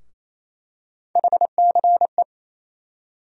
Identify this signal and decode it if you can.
Morse code, heard as a keyed tone, about 28 wpm, 700 Hz. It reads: HCE
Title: USA Call Sign Postfixes - 3 Letters 28wpm